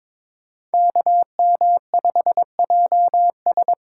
Morse code, heard as a keyed tone, about 26 wpm, 700 Hz. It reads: KM5JS
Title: Speed Racing - USA Call Signs 22wpm